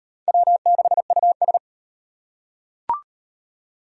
Morse code, heard as a keyed tone, about 34 wpm, 700 Hz. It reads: W6US